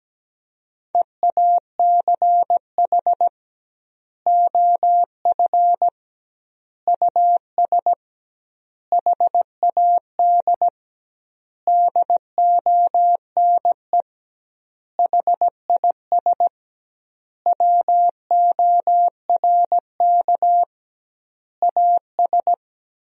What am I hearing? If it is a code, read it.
Morse code, heard as a keyed tone, about 17 wpm, 700 Hz. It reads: EACH OF US HAD DONE HIS WORK AS
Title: 2x Word-Spacing - Sentences from Top 500 Words - 17wpm